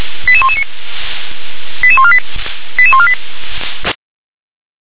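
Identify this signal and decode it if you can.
Five-tone selective call (ZVEI), heard as quick runs of five tones, tones of about 70 ms. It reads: 88119 88148 88148
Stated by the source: ZVEI five-tone selective call 88126 received by SDR.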